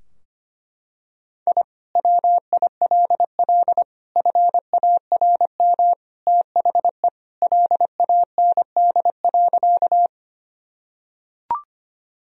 Morse code, heard as a keyed tone, about 25 wpm, 700 Hz. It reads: I WILL FARM THE LAND.
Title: Sentences from Top 1000 Words - Review Part 2 - 25wpm